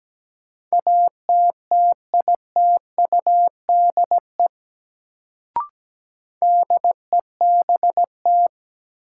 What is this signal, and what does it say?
Morse code, heard as a keyed tone, about 17 wpm, 700 Hz. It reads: ATTITUDE DEBT
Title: Sentences from Top 1300 Words - New Word Drill Alt - 17wpm